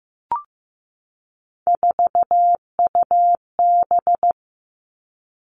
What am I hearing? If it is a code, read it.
Morse code, heard as a keyed tone, about 15 wpm, 700 Hz. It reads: 4UB
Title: All Cut Numbers - 3-Digit 15wpm